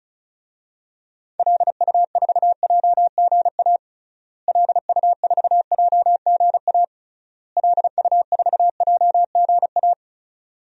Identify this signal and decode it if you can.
Morse code, heard as a keyed tone, about 34 wpm, 700 Hz. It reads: LU4JGA LU4JGA LU4JGA
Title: DXCC-100 Call Signs - 6 Characters - Repeated 3x - 35wpm